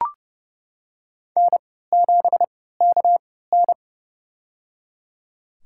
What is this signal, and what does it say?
Morse code, heard as a keyed tone, about 30 wpm, spaced as for 17 wpm, 700 Hz. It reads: N7KN
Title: USA Call Signs - 4 Characters - 3x Character Spacing - 30wpm